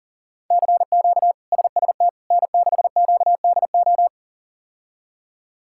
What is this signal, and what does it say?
Morse code, heard as a keyed tone, about 40 wpm, 700 Hz. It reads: CQ SST N6QDO
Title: K1USN SST QSO Practice - 40wpm